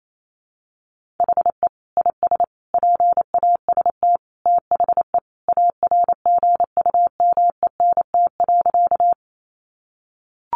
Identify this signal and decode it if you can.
Morse code, heard as a keyed tone, about 28 wpm, 700 Hz. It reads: HE IS PAST THE ARGUMENT.
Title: Sentences from Top 1400 Words 28wpm